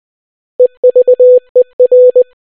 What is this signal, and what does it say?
Morse code, heard as a keyed tone, about 20 wpm, 500 Hz. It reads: EVER